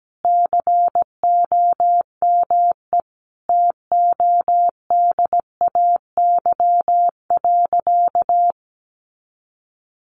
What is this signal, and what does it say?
Morse code, heard as a keyed tone, about 17 wpm, 700 Hz. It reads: COME TODAY.
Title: Easy Phrases 17wpm